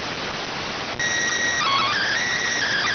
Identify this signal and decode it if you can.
SSTV (slow-scan television), PD 290.